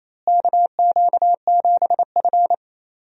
Morse code, heard as a keyed tone, about 28 wpm, 700 Hz. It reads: KQ7F
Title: USA Call Signs - 4 Characters - Repeated 2x - 28wpm